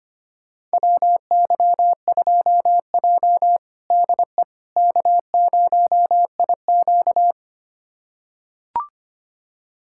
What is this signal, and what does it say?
Morse code, heard as a keyed tone, about 25 wpm, 700 Hz. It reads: WY2J DE K0IQ
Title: USA Callsign DE Callsign - 4 Characters - 25wpm